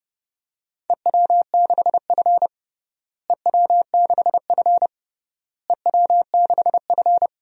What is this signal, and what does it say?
Morse code, heard as a keyed tone, about 32 wpm, 700 Hz. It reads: EW6F EW6F EW6F
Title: DXCC-100 Call Signs - 4 Characters - Repeated 3x - 30wpm